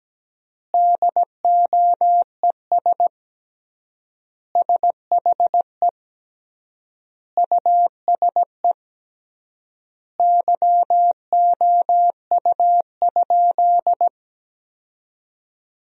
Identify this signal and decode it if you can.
Morse code, heard as a keyed tone, about 17 wpm, 700 Hz. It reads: DOES SHE USE YOU?
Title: 3x Word-Spacing - Sentences from Top 200 Words - 17wpm